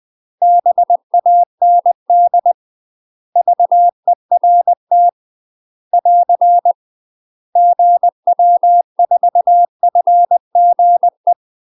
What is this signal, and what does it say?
Morse code, heard as a keyed tone, about 20 wpm, 700 Hz. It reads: BAND VERT <AR> GW4FGE
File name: UK MASTER QSOs TRAINING / 20WPM / UK MASTER QSOs 0060